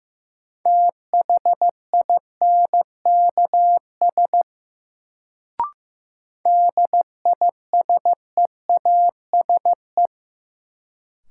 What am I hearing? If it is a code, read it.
Morse code, heard as a keyed tone, about 15 wpm, 700 Hz. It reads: THINKS DISEASE
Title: Sentences from Top 1000 Words - Review Part 6 - New Word Drill - 15wpm